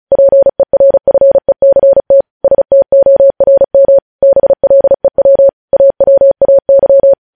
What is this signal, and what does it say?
Morse code, heard as a keyed tone, about 36 wpm, 550 Hz. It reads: PERFECT STORM BLEW AWAY